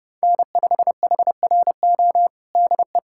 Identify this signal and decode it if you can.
Morse code, heard as a keyed tone, about 32 wpm, 700 Hz. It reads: N5HRO DE
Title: USA Callsign DE Callsign 30wpm